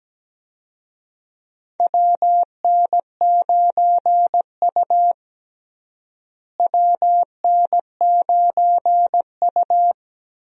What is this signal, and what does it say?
Morse code, heard as a keyed tone, about 17 wpm, 700 Hz. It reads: WN9U WN9U
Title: USA Call Signs - 4 Characters - Repeated 2x - 17wpm